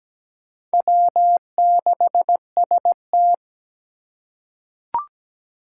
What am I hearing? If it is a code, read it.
Morse code, heard as a keyed tone, about 17 wpm, 700 Hz. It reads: W6ST
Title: USA Call Signs - 4 Characters - 3x Character Spacing - 17wpm